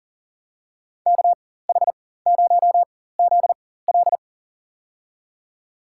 Morse code, heard as a keyed tone, about 40 wpm, spaced as for 19 wpm, 700 Hz. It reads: KH0ZL
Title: USA Call Signs - 5 Characters - 4x Character Spacing - 40wpm